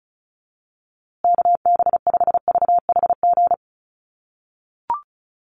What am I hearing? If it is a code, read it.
Morse code, heard as a keyed tone, about 34 wpm, 700 Hz. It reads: KB5VHG
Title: USA Call Signs Repeated 3x 35wpm